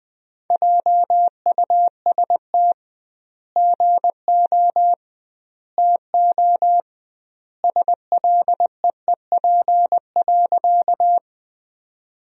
Morse code, heard as a keyed tone, about 20 wpm, 700 Hz. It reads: JUST GO TO SLEEP.
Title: 2x Word-Spacing - Sentences from Top 500 Words - 20wpm